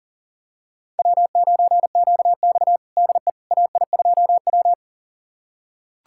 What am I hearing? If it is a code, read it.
Morse code, heard as a keyed tone, about 40 wpm, 700 Hz. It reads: W9QX DE AI2W
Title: USA Callsign DE Callsign - 4 Characters - 40wpm